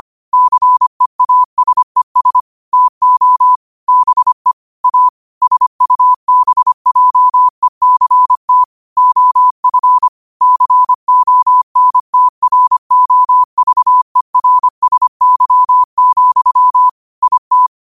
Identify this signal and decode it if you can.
Morse code, heard as a keyed tone, about 25 wpm, 1000 Hz. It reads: CEASES TO BE A SUBJECT OF CONTROVERSY, IT
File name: qotdfiles / 25wpm